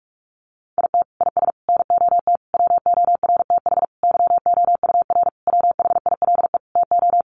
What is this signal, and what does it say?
Morse code, heard as a keyed tone, about 44 wpm, 700 Hz. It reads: IT IS NOT WORTH YOUR WHILE TO
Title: Sentences from Top 800 Words 45wpm